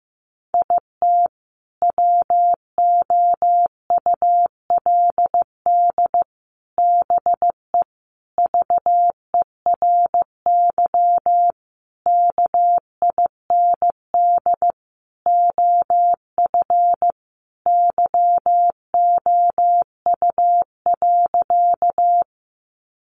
Morse code, heard as a keyed tone, about 15 wpm, 700 Hz. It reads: IT WOULD BE VERY KIND OF YOU.
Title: Sentences from Top 200 Words 15wpm